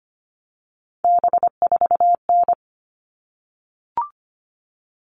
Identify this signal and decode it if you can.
Morse code, heard as a keyed tone, about 25 wpm, 700 Hz. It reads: B4N
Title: All Cut Numbers - 3-Digit 25wpm